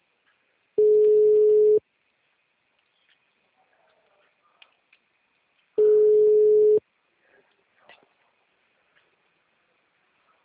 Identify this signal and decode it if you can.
Ringback tone (Europe, 425 Hz).
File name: out- -2005- - - .7